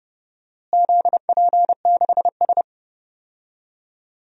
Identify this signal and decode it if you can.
Morse code, heard as a keyed tone, about 28 wpm, 700 Hz. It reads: ZP6S
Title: DXCC-100 Call Signs - 4 Characters - Repeated 3x - 30wpm